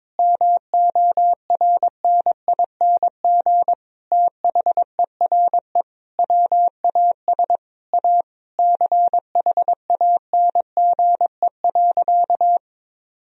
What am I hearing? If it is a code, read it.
Morse code, heard as a keyed tone, about 22 wpm, 700 Hz. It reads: MORNING THERE WAS A CHANGE.
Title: Sentences from Top 1000 Words - Review Part 1 - 22wpm